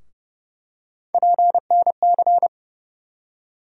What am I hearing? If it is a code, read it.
Morse code, heard as a keyed tone, about 30 wpm, 700 Hz. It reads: PNC
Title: USA Call Sign Postfixes - 3 Letters 30wpm